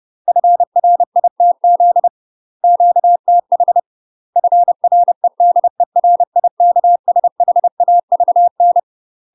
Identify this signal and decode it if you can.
Morse code, heard as a keyed tone, about 30 wpm, 700 Hz. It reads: FRITZ QTH FREDERIKSHAVN
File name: UK MASTER QSOs TRAINING / 30WPM / UK MASTER QSOs 0018